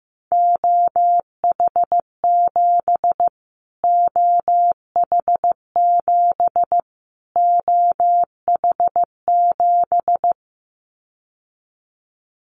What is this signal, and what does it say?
Morse code, heard as a keyed tone, about 15 wpm, 700 Hz. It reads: OH7 OH7 OH7
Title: Callsign Prefixes for Top 25 DXCC - Repeated 3x - 15wpm